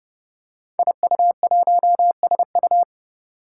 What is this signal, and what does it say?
Morse code, heard as a keyed tone, about 30 wpm, 700 Hz. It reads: IU1SU
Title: Five Challenging Letter-Number Combinations 30wpm